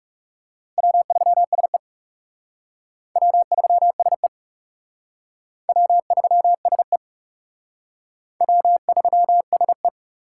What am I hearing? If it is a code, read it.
Morse code, heard as a keyed tone, about 36 wpm, 700 Hz. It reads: W3SE W3SE W3SE W3SE
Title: Speed Racing - USA Call Signs 30wpm